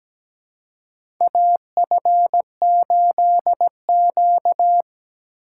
Morse code, heard as a keyed tone, about 17 wpm, 700 Hz. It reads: AF8Q
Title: USA Call Signs - 4x Character Spacing - 17wpm